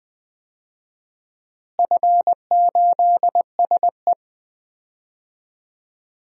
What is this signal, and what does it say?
Morse code, heard as a keyed tone, about 20 wpm, 700 Hz. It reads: F8SE
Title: DXCC-100 Call Signs - 4 Characters - Repeated 2x - 20wpm